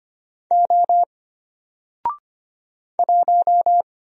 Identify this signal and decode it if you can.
Morse code, heard as a keyed tone, about 25 wpm, 700 Hz. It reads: O 1
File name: Single-Letter-Number-25wpm